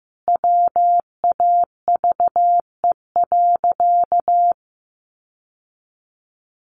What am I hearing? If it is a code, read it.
Morse code, heard as a keyed tone, about 15 wpm, 700 Hz. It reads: WAVE.
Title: Sentences from Top 1400 Words 15wpm